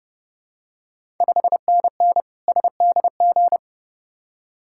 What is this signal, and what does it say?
Morse code, heard as a keyed tone, about 30 wpm, 700 Hz. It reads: 5NN SDG
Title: Ontario QSO Party Exchange 30wpm